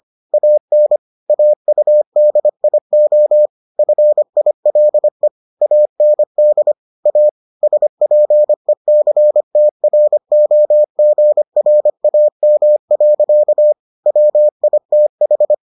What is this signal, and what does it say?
Morse code, heard as a keyed tone, about 25 wpm, 600 Hz. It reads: AN AUDIO FILE AND A SPECTROGRAM. WITH